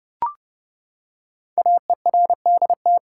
Morse code, heard as a keyed tone, about 30 wpm, 700 Hz. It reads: AERDT
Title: Code Groups - 5 Letters - 30wpm